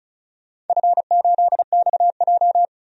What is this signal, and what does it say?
Morse code, heard as a keyed tone, about 36 wpm, 700 Hz. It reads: F8XJ
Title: DXCC-100 Call Signs - 4 Characters - 35wpm